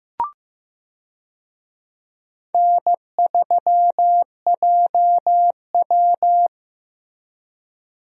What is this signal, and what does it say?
Morse code, heard as a keyed tone, about 15 wpm, 700 Hz. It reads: N3JW
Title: USA Call Signs - 4 Characters - Repeated 3x - 15wpm